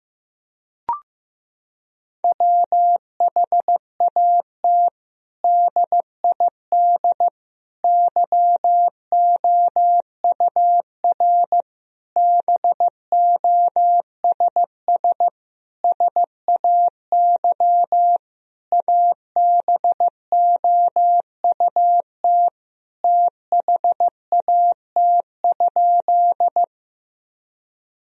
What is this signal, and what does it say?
Morse code, heard as a keyed tone, about 15 wpm, 700 Hz. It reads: WHAT DID YOUR BOSS SAY ABOUT THAT?
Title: Sentences from Top 1000 Words - Review Part 4 - 15wpm